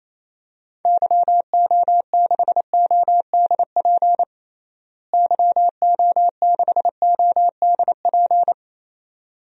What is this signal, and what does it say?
Morse code, heard as a keyed tone, about 28 wpm, 700 Hz. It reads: YO6ODP YO6ODP
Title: DXCC-100 Call Signs - 6 Characters - Repeated 2x - 28wpm